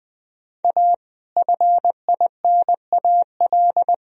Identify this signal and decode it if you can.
Morse code, heard as a keyed tone, about 21 wpm, 700 Hz. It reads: A FINAL
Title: Sentences from Top 1000 Words - Review Part 4 - 20wpm